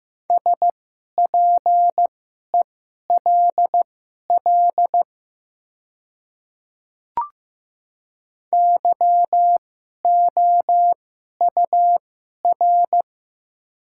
Morse code, heard as a keyed tone, about 15 wpm, spaced as for 11 wpm, 700 Hz. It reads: SPELL YOUR
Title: 2x Character Spacing - Top 500 Words - 15wpm